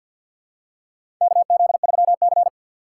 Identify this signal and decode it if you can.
Morse code, heard as a keyed tone, about 49 wpm, 700 Hz. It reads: KZ3C
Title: USA Call Signs - 4 Characters - 3x Character Spacing - 50wpm